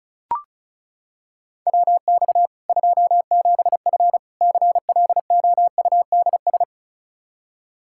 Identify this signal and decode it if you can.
Morse code, heard as a keyed tone, about 34 wpm, 700 Hz. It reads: WX 27F CLOUDS
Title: K1USN SST QSO Practice 35wpm